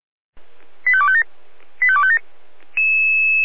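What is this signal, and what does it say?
Five-tone selective call (ZVEI), heard as quick runs of five tones, tones of about 70 ms. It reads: 85257 85257
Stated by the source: ZVEI five-tone selective call 85250 received by SDR, Feuerwehr Orlamünde audio server.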